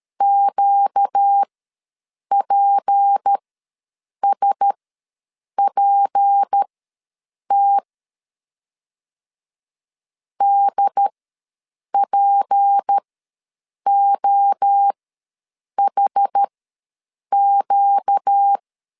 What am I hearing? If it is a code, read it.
Morse code, heard as a keyed tone, about 13 wpm, spaced as for 7 wpm, 800 Hz. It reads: QPSPT DPOHQ
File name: LEZIONI / 6 PKX0 / 6 P ws 12wpm 3117